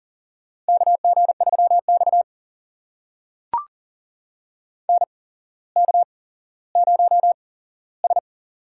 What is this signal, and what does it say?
Morse code, heard as a keyed tone, about 42 wpm, spaced as for 17 wpm, 700 Hz. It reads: KG3X NK0S
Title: USA Call Signs - 4 Characters - 8x Character Spacing - 40wpm